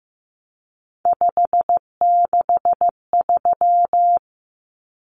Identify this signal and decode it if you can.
Morse code, heard as a keyed tone, about 15 wpm, 700 Hz. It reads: 563
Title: Numbers - 3-Digit 15wpm